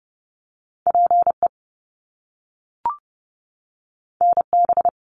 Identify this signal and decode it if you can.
Morse code, heard as a keyed tone, about 30 wpm, 700 Hz. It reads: PE NB
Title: Canadian Provinces and Territories 30wpm